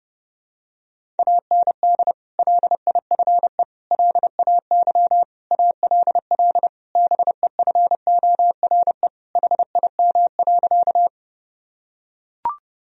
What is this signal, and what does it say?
Morse code, heard as a keyed tone, about 32 wpm, 700 Hz. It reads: AND LIFE LAY ALL BEFORE HIM.
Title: Sentences from Top 1000 Words - Review Part 2 - 30wpm